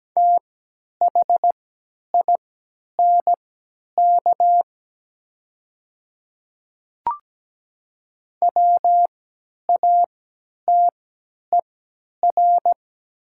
Morse code, heard as a keyed tone, about 17 wpm, spaced as for 10 wpm, 700 Hz. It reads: THINK WATER
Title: 3x Character Spacing - Top 200 Words - 17wpm